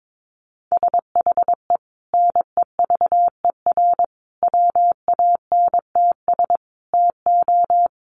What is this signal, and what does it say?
Morse code, heard as a keyed tone, about 22 wpm, 700 Hz. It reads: SHE NEVER WANTS TO
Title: Easy Phrases 22wpm